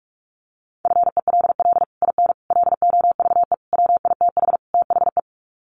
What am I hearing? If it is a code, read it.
Morse code, heard as a keyed tone, about 44 wpm, 700 Hz. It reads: FELL IN LOVE WITH THE
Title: Sets of 5 Words Encore2 45wpm